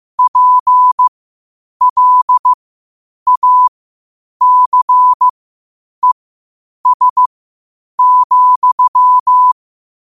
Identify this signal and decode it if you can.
Morse code, heard as a keyed tone, about 15 wpm, spaced as for 8 wpm, 1000 Hz. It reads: PLACES,
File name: qotdfiles / 8wpm